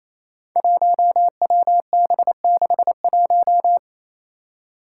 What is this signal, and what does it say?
Morse code, heard as a keyed tone, about 28 wpm, 700 Hz. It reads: 1WB61
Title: Five Challenging Letter-Number Combinations 28wpm